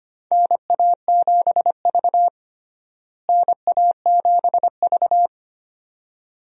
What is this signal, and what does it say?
Morse code, heard as a keyed tone, about 25 wpm, 700 Hz. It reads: NA7V NA7V
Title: USA Call Signs - 4 Characters - Repeated 2x - 25wpm